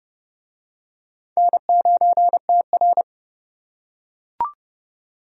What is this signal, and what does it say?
Morse code, heard as a keyed tone, about 29 wpm, 700 Hz. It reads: N9TR